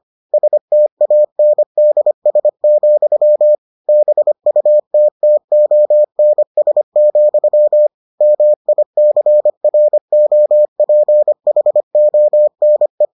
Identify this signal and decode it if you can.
Morse code, heard as a keyed tone, about 25 wpm, 600 Hz. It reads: STANDS, BUTTONS, MICROPHONE